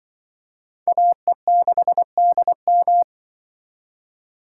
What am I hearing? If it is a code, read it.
Morse code, heard as a keyed tone, about 24 wpm, 700 Hz. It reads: AE6DM
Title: Speed Racing - USA 2x2 Call Signs 20wpm